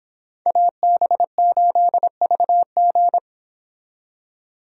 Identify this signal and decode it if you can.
Morse code, heard as a keyed tone, about 31 wpm, 700 Hz. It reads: AB8VG